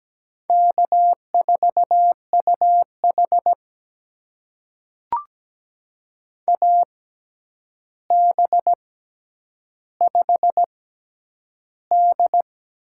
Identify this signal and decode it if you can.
Morse code, heard as a keyed tone, about 17 wpm, spaced as for 9 wpm, 700 Hz. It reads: K4UH AB5D
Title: USA Call Signs - 4 Characters - 6x Character Spacing - 17wpm